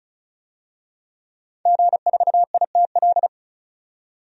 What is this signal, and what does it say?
Morse code, heard as a keyed tone, about 34 wpm, 700 Hz. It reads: G4ITL
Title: DXCC-100 Call Signs - 5 Characters - 35wpm